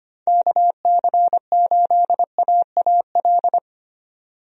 Morse code, heard as a keyed tone, about 25 wpm, 700 Hz. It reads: KC8AAL
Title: USA Call Signs - 6 Characters - 25wpm